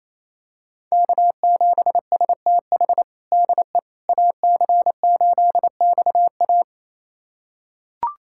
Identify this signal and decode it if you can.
Morse code, heard as a keyed tone, about 28 wpm, 700 Hz. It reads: K7STH DE AC8XA
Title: USA Callsign DE Callsign - 5 Characters - 28wpm